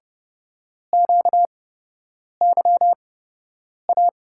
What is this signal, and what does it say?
Morse code, heard as a keyed tone, about 30 wpm, spaced as for 8 wpm, 700 Hz. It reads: QYA
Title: USA Call Signs - 6 Characters - 8x Character Spacing - 30wpm